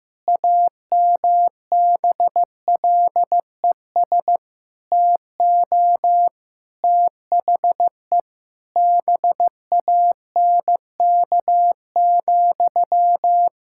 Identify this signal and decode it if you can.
Morse code, heard as a keyed tone, about 15 wpm, 700 Hz. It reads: AMBLES TO THE BANK,
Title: Pangram Challenge v2 - 15wpm